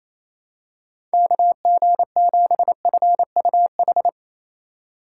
Morse code, heard as a keyed tone, about 28 wpm, 700 Hz. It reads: KG7FUH
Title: USA Call Signs - 6 Characters - 4x Character Spacing - 28wpm